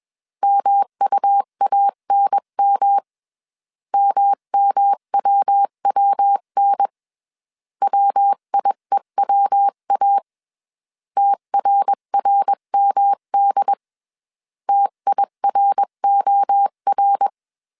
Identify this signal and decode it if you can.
Morse code, heard as a keyed tone, about 21 wpm, 800 Hz. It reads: MUANM MMWWN WIEWA TRRMD TIROR
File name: CW / DET / 1 3 1500C S B 20wpm 1754